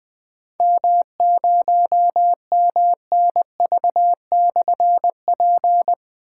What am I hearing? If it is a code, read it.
Morse code, heard as a keyed tone, about 20 wpm, 700 Hz. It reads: M0MNV/P
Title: Difficult Call Signs 20wpm